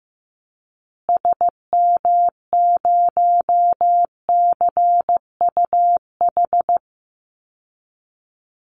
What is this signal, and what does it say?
Morse code, heard as a keyed tone, about 15 wpm, 700 Hz. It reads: SM0CUH
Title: Call Signs Repeated 3x - 15wpm